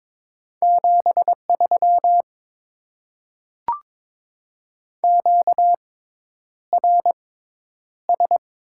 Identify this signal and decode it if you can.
Morse code, heard as a keyed tone, about 21 wpm, spaced as for 10 wpm, 700 Hz. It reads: 73 QRS
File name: Top 90 QSO Elements - 6x Character Spacing - 22wpm